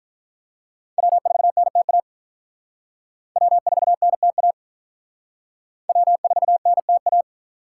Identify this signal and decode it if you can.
Morse code, heard as a keyed tone, about 44 wpm, 700 Hz. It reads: W4NTA W4NTA W4NTA